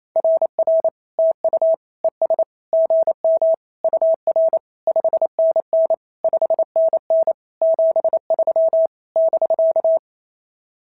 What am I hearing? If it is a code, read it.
Morse code, heard as a keyed tone, about 28 wpm, 650 Hz. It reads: RR TU ES GM UR 5NN 5NN 73 <BK>
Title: SOTA QSO Practice 28wpm